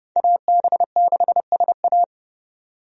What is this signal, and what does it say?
Morse code, heard as a keyed tone, about 29 wpm, 700 Hz. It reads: AB6SA
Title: USA Call Signs - 5 Characters - 30wpm